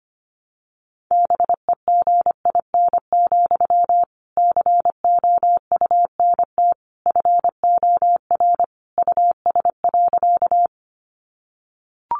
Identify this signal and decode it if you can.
Morse code, heard as a keyed tone, about 25 wpm, 700 Hz. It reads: BEGIN, COUNT FOR US.
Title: Sentences from Top 1500 Words 25wpm